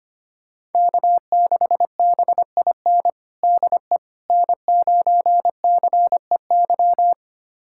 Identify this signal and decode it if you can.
Morse code, heard as a keyed tone, about 25 wpm, 700 Hz. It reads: K6BIN DE N9CEY
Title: USA Callsign DE Callsign - 5 Characters - 25wpm